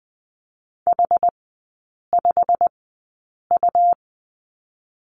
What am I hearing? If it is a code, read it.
Morse code, heard as a keyed tone, about 20 wpm, spaced as for 8 wpm, 700 Hz. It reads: H5U
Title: 127 Lesson 21e - Number 6 - Three Characters 20wpm